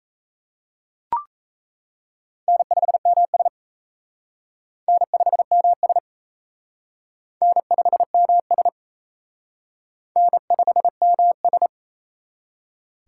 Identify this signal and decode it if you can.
Morse code, heard as a keyed tone, about 34 wpm, 700 Hz. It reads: N5MS N5MS N5MS N5MS
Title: Speed Racing - USA 1x2 Call Signs 28wpm